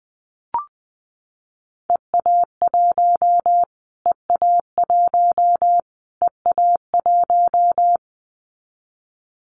Morse code, heard as a keyed tone, about 19 wpm, 700 Hz. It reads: EA1 EA1 EA1
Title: Callsign Prefixes for Top 25 DXCC - Repeated 3x - 20wpm